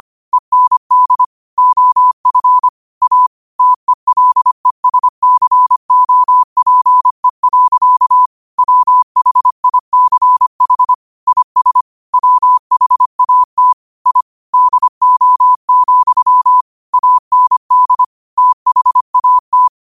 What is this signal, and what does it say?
Morse code, heard as a keyed tone, about 25 wpm, 1000 Hz. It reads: END OF A TELESCOPE. WHICH IS WHAT I DO, AND THAT
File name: qotdfiles / 25wpm